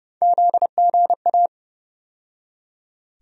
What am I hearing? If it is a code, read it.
Morse code, heard as a keyed tone, about 31 wpm, 700 Hz. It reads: ZGA
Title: USA Call Sign Postfixes - 3 Letters 30wpm